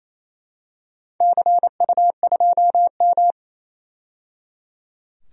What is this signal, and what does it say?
Morse code, heard as a keyed tone, about 27 wpm, 700 Hz. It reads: CU2M
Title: DXCC-100 Call Signs - 4 Characters - 28wpm